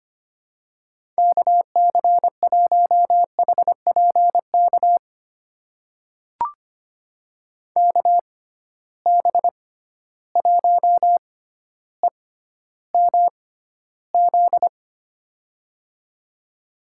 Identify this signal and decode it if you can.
Morse code, heard as a keyed tone, about 25 wpm, spaced as for 13 wpm, 700 Hz. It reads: KC1HPK KB1EMZ
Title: USA Call Signs - 6 Characters - 6x Character Spacing - 25wpm